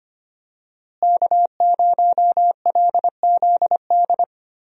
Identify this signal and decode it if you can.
Morse code, heard as a keyed tone, about 25 wpm, 700 Hz. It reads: K0LZD
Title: USA Call Signs - 5 Characters - Repeated 3x - 25wpm